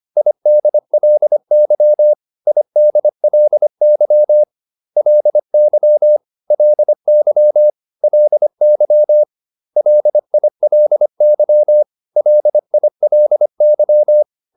Morse code, heard as a keyed tone, about 25 wpm, 600 Hz. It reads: IDLY IDLY LY LY LY LILY LILY